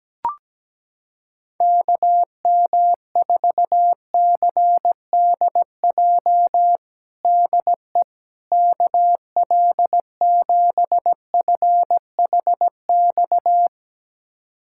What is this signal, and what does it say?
Morse code, heard as a keyed tone, about 17 wpm, 700 Hz. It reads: KM4CDJ DE KL7FHX
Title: USA Callsign DE Callsign - 6 Characters - 17wpm